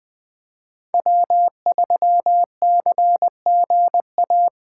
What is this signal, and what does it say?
Morse code, heard as a keyed tone, about 20 wpm, 700 Hz. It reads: W3CGA
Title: Speed Racing - USA 1x3 Call Signs 17wpm